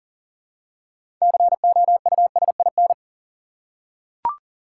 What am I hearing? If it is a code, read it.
Morse code, heard as a keyed tone, about 38 wpm, 700 Hz. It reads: COUSIN
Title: Six Letter Words - 40wpm